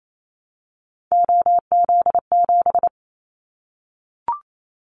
Morse code, heard as a keyed tone, about 28 wpm, 700 Hz. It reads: OZ7
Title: Callsign Prefixes for Top 100 DXCC - 28wpm